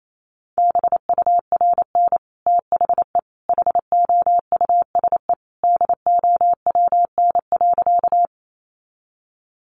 Morse code, heard as a keyed tone, about 28 wpm, 700 Hz. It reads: BURN THE HOUSE DOWN.
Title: Sentences from Top 1300 Words 28wpm